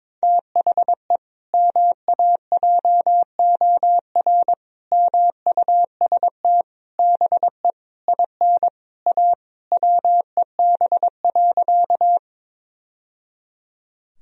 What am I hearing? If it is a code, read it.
Morse code, heard as a keyed tone, about 22 wpm, 700 Hz. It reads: THE MAJOR MUST BE IN A WEB.
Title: Sentences from Top 1000 Words - Review Part 4 - 22wpm